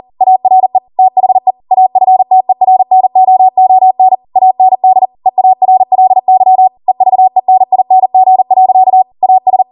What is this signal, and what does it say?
Morse code, heard as a keyed tone, about 40 wpm, 750 Hz. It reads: ARE THE AFTERNOON AND EARLY EVENING. AS